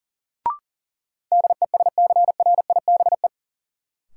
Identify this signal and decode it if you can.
Morse code, heard as a keyed tone, about 40 wpm, 700 Hz. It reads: DESCRIBE
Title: Sentences from Top 1000 Words - New Word Drill - 40wpm